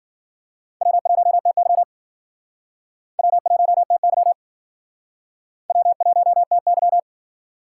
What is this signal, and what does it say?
Morse code, heard as a keyed tone, about 54 wpm, 700 Hz. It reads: W1TY W1TY W1TY